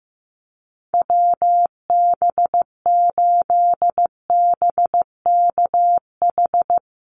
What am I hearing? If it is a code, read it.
Morse code, heard as a keyed tone, about 15 wpm, 700 Hz. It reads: WB8BKH
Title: USA Call Signs Repeated 3x 15wpm